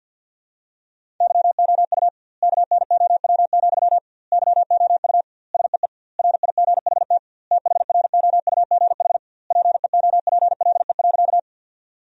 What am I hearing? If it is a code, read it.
Morse code, heard as a keyed tone, about 50 wpm, 700 Hz. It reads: YOU KNOW, YOU SEE RIGHT THROUGH PEOPLE.
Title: 2x Word-Spacing - Sentences from Top 200 Words - 50wpm